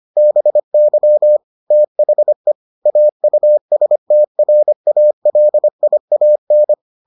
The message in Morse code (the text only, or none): BY THE AUSTRALIAN